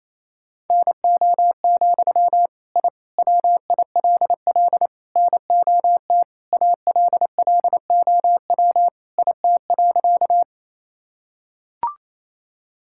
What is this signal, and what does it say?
Morse code, heard as a keyed tone, about 28 wpm, 700 Hz. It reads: NO, I WILL NOT ALLOW IT.
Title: Sentences from Top 1000 Words - Review Part 5 - 28wpm